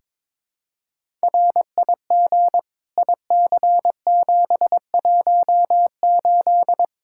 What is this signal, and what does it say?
Morse code, heard as a keyed tone, about 22 wpm, 700 Hz. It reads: RIG IC718
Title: Practice QSO - Exchange 2 - Info - 22wpm